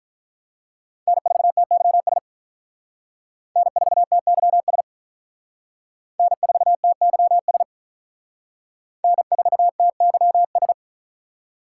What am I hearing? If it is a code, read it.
Morse code, heard as a keyed tone, about 43 wpm, 700 Hz. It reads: N4TYS N4TYS N4TYS N4TYS
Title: Speed Racing - USA 1x3 Call Signs 35wpm